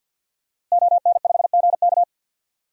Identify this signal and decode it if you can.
Morse code, heard as a keyed tone, about 52 wpm, 700 Hz. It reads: ON5GK
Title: DXCC-100 Call Signs - 5 Characters - Repeated 3x - 50wpm